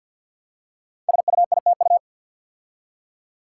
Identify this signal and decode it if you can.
Morse code, heard as a keyed tone, about 50 wpm, 690 Hz. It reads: SUITU
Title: Code Groups - 5 Letters - 50wpm